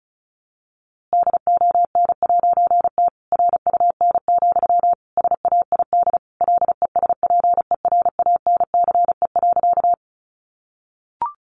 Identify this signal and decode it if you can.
Morse code, heard as a keyed tone, about 34 wpm, 700 Hz. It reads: DON'T RUN, SAID LESPERANCE.
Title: A Sound Of Thunder 35wpm